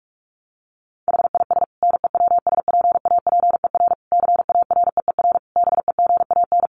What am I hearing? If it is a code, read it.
Morse code, heard as a keyed tone, about 44 wpm, 700 Hz. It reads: HIS NEWSPAPER CAREER BEGAN
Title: Sentences from Top 1100 Words 45wpm